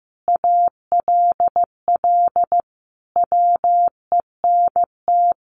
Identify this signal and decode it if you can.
Morse code, heard as a keyed tone, about 15 wpm, 700 Hz. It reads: ALL WENT
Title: Sentences from Top 1300 Words 15wpm